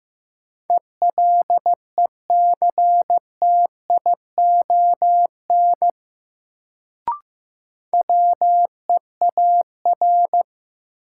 Morse code, heard as a keyed tone, about 15 wpm, 700 Hz. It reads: ELECTION WEAR
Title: Sentences from Top 1500 Words - New Word Drill Alt - 15wpm